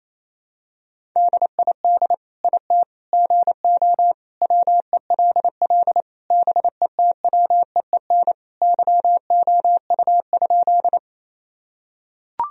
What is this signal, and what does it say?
Morse code, heard as a keyed tone, about 27 wpm, 700 Hz. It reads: DID IT GO WELL BETWEEN YOU?
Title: Sentences from Top 1000 Words - Review Part 1 - 28wpm